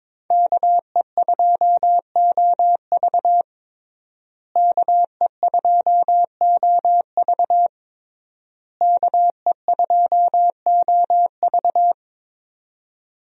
Morse code, heard as a keyed tone, about 22 wpm, 700 Hz. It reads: KE2OV KE2OV KE2OV
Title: USA Call Signs - 5 Characters - Repeated 3x - 22wpm